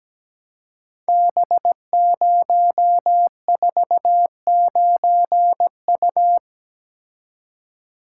Morse code, heard as a keyed tone, about 17 wpm, 700 Hz. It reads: B049U
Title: Code Groups - 5 Letter:Numbers - 17wpm